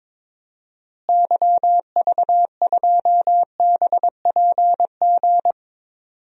Morse code, heard as a keyed tone, about 22 wpm, 700 Hz. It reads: YV2BPG
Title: DXCC-100 Call Signs - 6 Characters - Repeated 2x - 22wpm